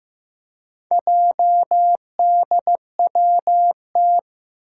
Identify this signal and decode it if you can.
Morse code, heard as a keyed tone, about 15 wpm, 700 Hz. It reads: JDWT
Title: Four Challenging Letter-Number Combinations 15wpm